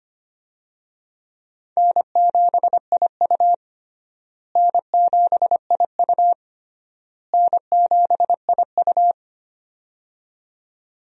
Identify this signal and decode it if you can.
Morse code, heard as a keyed tone, about 25 wpm, 700 Hz. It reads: N7IU N7IU N7IU
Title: USA Call Signs - 4 Characters - Repeated 3x - 25wpm